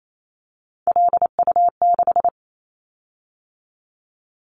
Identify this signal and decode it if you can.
Morse code, heard as a keyed tone, about 28 wpm, 700 Hz. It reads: LU6
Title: Callsign Prefixes for Top 50 DXCC - 28wpm